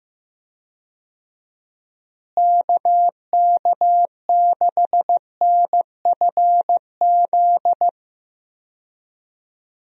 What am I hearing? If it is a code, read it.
Morse code, heard as a keyed tone, about 15 wpm, 700 Hz. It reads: KK6NFZ
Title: USA Call Signs - 6 Characters - Repeated 2x - 15wpm